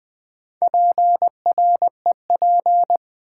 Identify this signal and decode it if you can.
Morse code, heard as a keyed tone, about 20 wpm, 700 Hz. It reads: PREP
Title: Build A Word - 4 Letters At A Time - 8-Letter Words - 20wpm